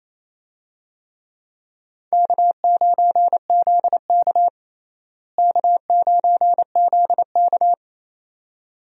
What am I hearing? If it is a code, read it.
Morse code, heard as a keyed tone, about 28 wpm, 700 Hz. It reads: K9ZK K9ZK
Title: USA Call Signs - 4 Characters - Repeated 2x - 28wpm